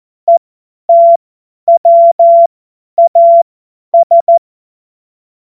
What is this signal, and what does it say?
Morse code, heard as a keyed tone, about 14 wpm, spaced as for 10 wpm, 680 Hz. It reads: ETWAS
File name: Morsekurs / Witze / Witz 28. September 2021, 50 F70